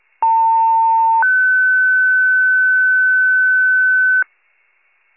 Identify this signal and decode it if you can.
Two-tone page, heard as a one-second tone, then a three-second tone, about 900 then 1500 Hz. It is heard from 0.2 to 4.2 s.